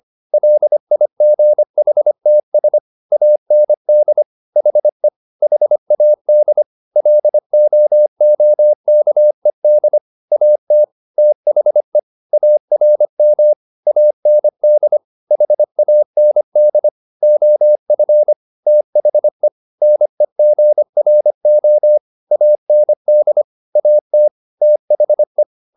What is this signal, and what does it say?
Morse code, heard as a keyed tone, about 25 wpm, 600 Hz. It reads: LIGHTS AND HE HAD LOOKED AT THE ARM AND HAND OF THE NEGRO AND AT THE